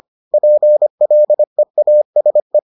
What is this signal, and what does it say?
Morse code, heard as a keyed tone, about 25 wpm, 600 Hz. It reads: PLEASE